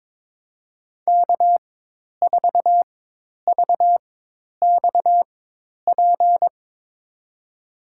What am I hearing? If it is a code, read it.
Morse code, heard as a keyed tone, about 22 wpm, spaced as for 10 wpm, 700 Hz. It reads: K4VXP
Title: USA Call Signs - 5 Characters - 4x Character Spacing - 22wpm